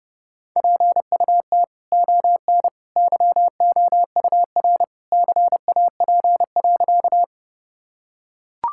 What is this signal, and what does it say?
Morse code, heard as a keyed tone, about 30 wpm, 700 Hz. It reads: PUT ON YOUR CAP.
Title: Sentences from Top 1600 Words 30wpm